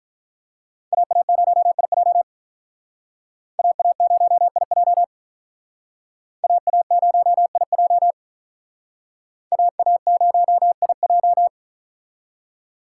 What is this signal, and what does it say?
Morse code, heard as a keyed tone, about 44 wpm, 700 Hz. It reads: AA0IJ AA0IJ AA0IJ AA0IJ